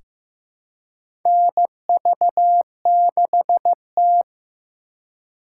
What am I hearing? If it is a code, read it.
Morse code, heard as a keyed tone, about 15 wpm, 700 Hz. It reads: NV6T
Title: USA Call Signs - 4 Characters - 2x Character Spacing - 15wpm